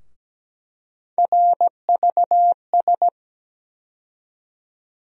Morse code, heard as a keyed tone, about 17 wpm, 700 Hz. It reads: RVS
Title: USA Call Sign Postfixes - 3 Letters 17wpm